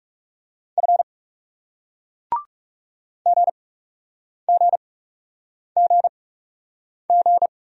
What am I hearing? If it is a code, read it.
Morse code, heard as a keyed tone, about 40 wpm, spaced as for 8 wpm, 700 Hz. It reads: F GGGG